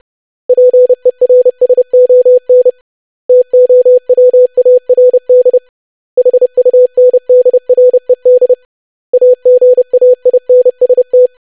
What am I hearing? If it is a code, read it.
Morse code, heard as a keyed tone, about 30 wpm, 500 Hz. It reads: PERSON TOWARD HUNDRED AGAINST
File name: lesson / web / basic500-25wpm